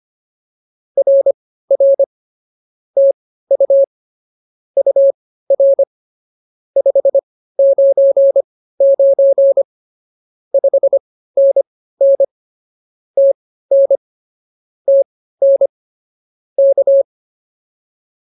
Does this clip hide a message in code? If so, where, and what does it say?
Yes: Morse code from 1.0 to 17.0 s, heard as a keyed tone, about 24 wpm, spaced as for 15 wpm, 550 Hz. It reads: RR TU UR 599 5NN TN TN K